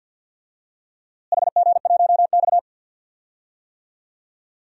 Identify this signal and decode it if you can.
Morse code, heard as a keyed tone, about 49 wpm, 700 Hz. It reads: HG1X
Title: DXCC-100 Call Signs - 4 Characters - Repeated 2x - 50wpm